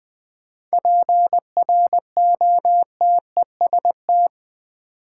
Morse code, heard as a keyed tone, about 20 wpm, 700 Hz. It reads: PROTEST
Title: 7 Letter Words - First 3 Letters Given - 20wpm